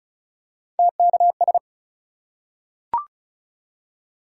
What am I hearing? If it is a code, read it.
Morse code, heard as a keyed tone, about 36 wpm, 700 Hz. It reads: TKS
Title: Top 180 QSO Elements 35wpm